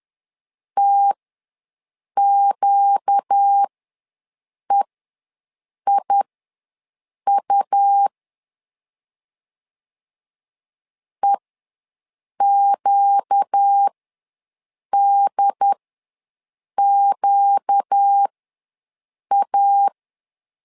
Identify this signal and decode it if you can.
Morse code, heard as a keyed tone, about 10 wpm, spaced as for 6 wpm, 800 Hz. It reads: TQEIU EQDQA
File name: LEZIONI / 4 Q ws 10wpm 2306